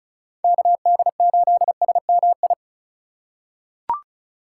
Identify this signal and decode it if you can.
Morse code, heard as a keyed tone, about 36 wpm, 690 Hz. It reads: KD8SMI
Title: USA Call Signs - 6 Characters - 3x Character Spacing - 35wpm